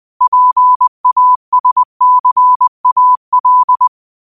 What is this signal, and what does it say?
Morse code, heard as a keyed tone, about 20 wpm, 1000 Hz. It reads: PASCAL